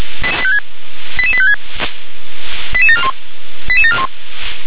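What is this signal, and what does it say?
Five-tone selective call (ZVEI), heard as quick runs of five tones, tones of about 70 ms. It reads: 88856 88741 88741
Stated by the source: ZVEI five-tone call received on an SDR.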